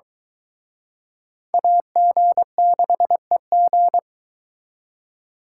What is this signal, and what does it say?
Morse code, heard as a keyed tone, about 23 wpm, 700 Hz. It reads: AG6EG